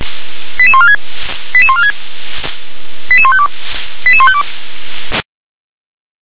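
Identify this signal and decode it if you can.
Five-tone selective call (ZVEI), heard as quick runs of five tones, tones of about 70 ms. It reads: 88146 88146 88152 88152
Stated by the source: ZVEI five-tone call 88081 received by SDR.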